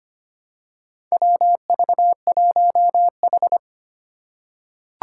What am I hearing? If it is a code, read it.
Morse code, heard as a keyed tone, about 25 wpm, 700 Hz. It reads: WV1H
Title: Four Challenging Letter-Number Combinations 25wpm